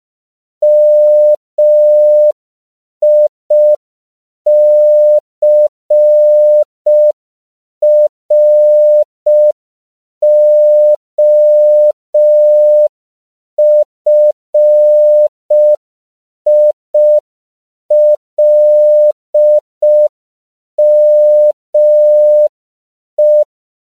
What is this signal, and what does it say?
Morse code, heard as a keyed tone, about 5 wpm, 600 Hz. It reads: MICROFILME